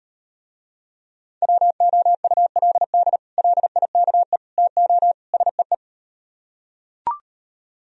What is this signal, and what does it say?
Morse code, heard as a keyed tone, about 38 wpm, 700 Hz. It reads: WOULD LIKE TO SEE